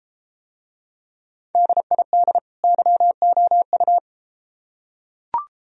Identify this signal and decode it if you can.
Morse code, heard as a keyed tone, about 33 wpm, 700 Hz. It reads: DID YOU